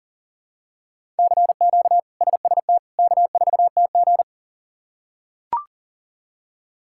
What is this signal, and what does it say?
Morse code, heard as a keyed tone, about 40 wpm, 700 Hz. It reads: CQ SST K4TG
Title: K1USN SST QSO Practice - 40wpm